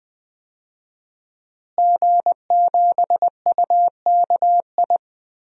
Morse code, heard as a keyed tone, about 20 wpm, 700 Hz. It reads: G7UKI
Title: DXCC-100 Call Signs - 5 Characters - Repeated 3x - 20wpm